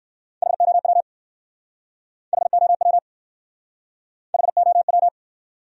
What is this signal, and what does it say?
Morse code, heard as a keyed tone, about 59 wpm, 700 Hz. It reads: HOW HOW HOW